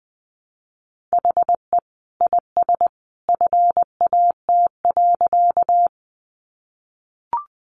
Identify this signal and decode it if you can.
Morse code, heard as a keyed tone, about 20 wpm, 700 Hz. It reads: HE IS FAT.
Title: Easy Phrases 20wpm